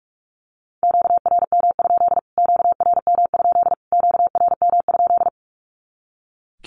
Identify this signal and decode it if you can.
Morse code, heard as a keyed tone, about 44 wpm, 700 Hz. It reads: QRM? QRM? QRM?
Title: Q-Codes Repeated 3x 45wpm